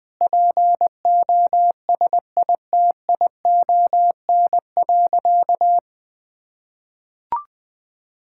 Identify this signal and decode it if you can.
Morse code, heard as a keyed tone, about 20 wpm, 700 Hz. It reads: POSITION.
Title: Sentences from Top 1000 Words - Review Part 5 - 20wpm